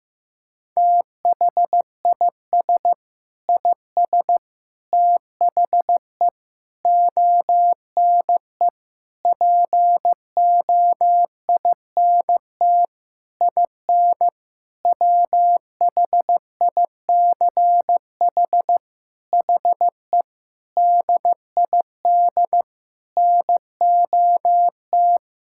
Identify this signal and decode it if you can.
Morse code, heard as a keyed tone, about 15 wpm, 700 Hz. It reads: THIS IS THE ONE POINT IN WHICH HE DID NOT
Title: Sentences from Top 1000 Words - Review Part 5 - 15wpm